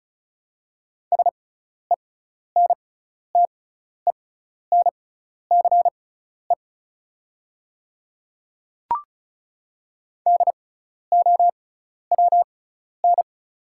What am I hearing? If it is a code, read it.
Morse code, heard as a keyed tone, about 34 wpm, spaced as for 12 wpm, 700 Hz. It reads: SENTENCE DOWN
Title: 6x Character Spacing - Top 300 Words - 35wpm